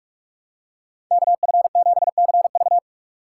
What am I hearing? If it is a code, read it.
Morse code, heard as a keyed tone, about 43 wpm, 700 Hz. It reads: KF7CV
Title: USA Call Signs - 4x Character Spacing - 45wpm